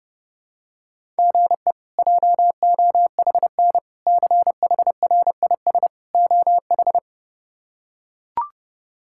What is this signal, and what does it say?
Morse code, heard as a keyed tone, about 30 wpm, 700 Hz. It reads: GE JOHN CHRIS OH
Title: K1USN SST QSO Practice - 30wpm